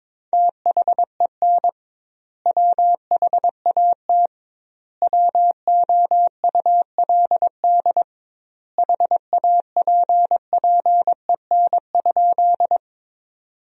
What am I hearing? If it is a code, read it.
Morse code, heard as a keyed tone, about 23 wpm, 700 Hz. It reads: THEN WHAT WOULD HAPPEN?
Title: 2x Word-Spacing - Sentences from Top 400 Words - 22wpm